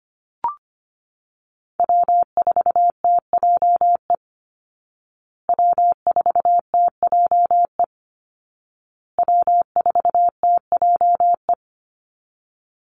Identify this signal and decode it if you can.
Morse code, heard as a keyed tone, about 25 wpm, 700 Hz. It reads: W4TJE W4TJE W4TJE
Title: Call Signs Repeated 2x - 25wpm